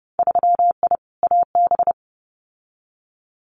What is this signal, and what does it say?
Morse code, heard as a keyed tone, about 30 wpm, 700 Hz. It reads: 3I AB
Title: Winter Field Day Exchanges Encore 30wpm